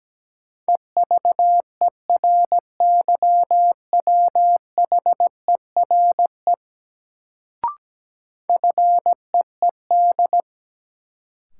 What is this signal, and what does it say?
Morse code, heard as a keyed tone, about 17 wpm, 700 Hz. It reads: EVERYWHERE FEED
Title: Sentences from Top 1600 Words - New Word Drill Alt - 17wpm